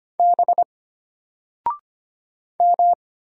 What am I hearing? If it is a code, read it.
Morse code, heard as a keyed tone, about 25 wpm, 700 Hz. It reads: B M
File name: Single-Letter-Number-25wpm